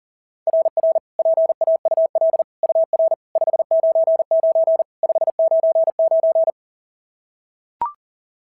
Morse code, heard as a keyed tone, about 41 wpm, 690 Hz. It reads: RR PAUL UR 599 599
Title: SOTA QSO Practice 40wpm